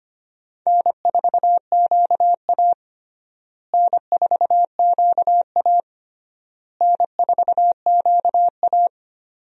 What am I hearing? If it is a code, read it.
Morse code, heard as a keyed tone, about 25 wpm, 700 Hz. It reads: N4QA N4QA N4QA
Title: USA Call Signs - 4 Characters - Repeated 3x - 25wpm